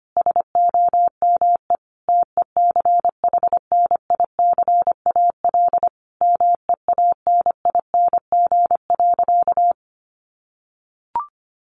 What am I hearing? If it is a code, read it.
Morse code, heard as a keyed tone, about 25 wpm, 700 Hz. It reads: SOME TECHNICAL MEANING.
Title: Sentences from Top 1100 Words 25wpm